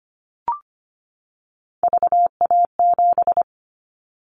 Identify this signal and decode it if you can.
Morse code, heard as a keyed tone, about 25 wpm, 700 Hz. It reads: VA7
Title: Callsign Prefixes for Top 50 DXCC - 25wpm